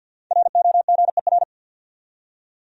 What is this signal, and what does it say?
Morse code, heard as a keyed tone, about 51 wpm, 700 Hz. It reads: ROGER
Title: USA Common Names - Max 5 Letters - 50wpm